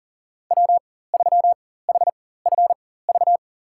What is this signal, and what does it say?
Morse code, heard as a keyed tone, about 39 wpm, spaced as for 19 wpm, 700 Hz. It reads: W3HFV
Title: USA Call Signs - 5 Characters - 4x Character Spacing - 40wpm